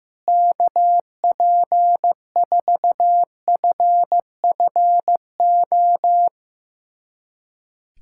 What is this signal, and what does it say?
Morse code, heard as a keyed tone, about 16 wpm, 700 Hz. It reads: KP4FFO
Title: USA Callsign DE Callsign - 6 Characters - 15wpm